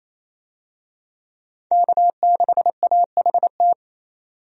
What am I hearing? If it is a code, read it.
Morse code, heard as a keyed tone, about 28 wpm, 700 Hz. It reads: K6AHT
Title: USA Call Signs - 5 Characters - 28wpm